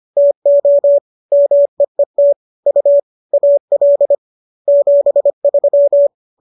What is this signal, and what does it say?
Morse code, heard as a keyed tone, about 25 wpm, 570 Hz. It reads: TO MEET U AL 73